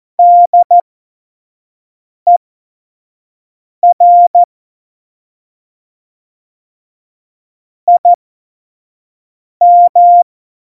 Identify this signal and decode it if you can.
Morse code, heard as a keyed tone, about 14 wpm, spaced as for 5 wpm, 700 Hz. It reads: DER IM
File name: Morsekurs / Witze / Witz 26. Februar 2019, 25 F70